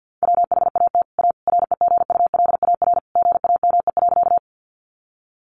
Morse code, heard as a keyed tone, about 50 wpm, 700 Hz. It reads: WHAT A REGULAR GAME.
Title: Sentences from Top 900 Words 50wpm